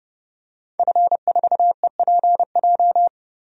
Morse code, heard as a keyed tone, about 30 wpm, 690 Hz. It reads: F4EPJ
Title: DXCC-100 Call Signs - 5 Characters - 30wpm